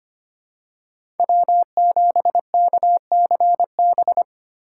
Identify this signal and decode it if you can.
Morse code, heard as a keyed tone, about 25 wpm, 700 Hz. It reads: W7KCB
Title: USA Call Signs - 5 Characters - 3x Character Spacing - 25wpm